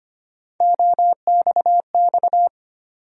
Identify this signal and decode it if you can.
Morse code, heard as a keyed tone, about 25 wpm, 700 Hz. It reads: OXX
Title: USA Call Sign Postfixes - 3 Letters 25wpm